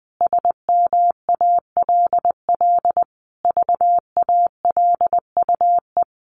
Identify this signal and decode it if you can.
Morse code, heard as a keyed tone, about 20 wpm, 700 Hz. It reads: SMALL VALUE
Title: Sentences from Top 1500 Words 20wpm